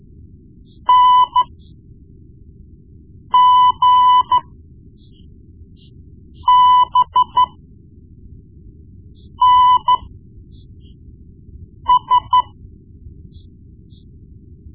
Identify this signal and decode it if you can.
Morse code, heard as a keyed tone, about 14 wpm, spaced as for 4 wpm, 1000 Hz. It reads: NGBNS
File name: Morse Lessons / Morse Lessons / Tape-1B